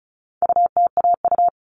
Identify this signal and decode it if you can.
Morse code, heard as a keyed tone, about 34 wpm, 700 Hz. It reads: UTAU
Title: All Cut Numbers - 4-Digit 35wpm